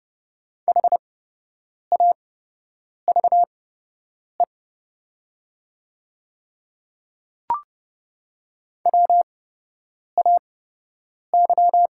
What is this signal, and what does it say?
Morse code, heard as a keyed tone, about 29 wpm, spaced as for 8 wpm, 700 Hz. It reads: HAVE WAY
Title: 8x Character Spacing - Top 100 Words - 30wpm